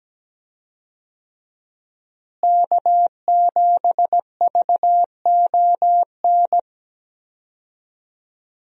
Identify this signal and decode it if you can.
Morse code, heard as a keyed tone, about 17 wpm, 700 Hz. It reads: K7VON
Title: USA Call Signs - 5 Characters - Repeated 2x - 17wpm